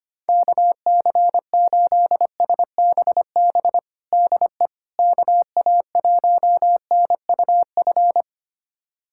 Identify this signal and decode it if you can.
Morse code, heard as a keyed tone, about 25 wpm, 700 Hz. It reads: KC8SBB DE KA1NUF
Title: USA Callsign DE Callsign - 6 Characters - 25wpm